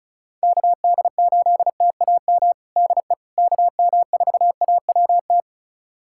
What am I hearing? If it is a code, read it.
Morse code, heard as a keyed tone, about 36 wpm, 700 Hz. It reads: KD8TAM DE KM4AWT
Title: USA Callsign DE Callsign - 6 Characters - 35wpm